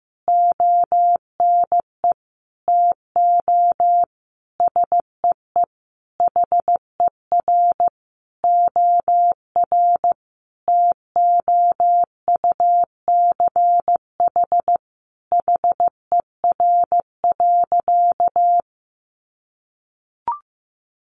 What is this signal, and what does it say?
Morse code, heard as a keyed tone, about 15 wpm, 700 Hz. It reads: ONE TO SEE HER OR TOUCH HER.
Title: Sentences from Top 1200 Words 15wpm